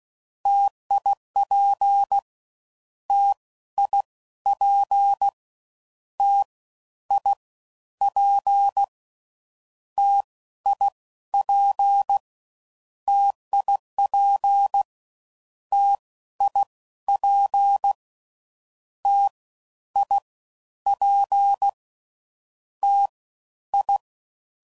Morse code, heard as a keyed tone, about 16 wpm, 800 Hz. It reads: TIP T I P T I P T I P TIP T I P T I P T I
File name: Combinations / Difficult Combos / T AOEI's